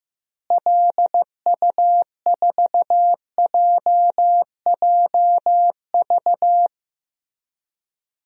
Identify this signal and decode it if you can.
Morse code, heard as a keyed tone, about 15 wpm, 700 Hz. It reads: LU4JJV
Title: DXCC-100 Call Signs - 6 Characters - Repeated 3x - 15wpm